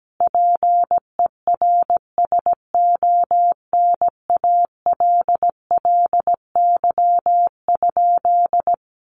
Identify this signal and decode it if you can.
Morse code, heard as a keyed tone, about 17 wpm, 700 Hz. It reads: PERSONALLY?
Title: Sentences from Top 1400 Words 17wpm